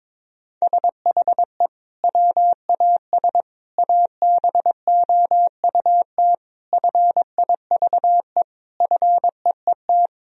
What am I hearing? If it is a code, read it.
Morse code, heard as a keyed tone, about 22 wpm, 700 Hz. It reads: SHE WAS ABOUT FIVE FEET
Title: Sentences from Top 1000 Words - Review Part 2 - 22wpm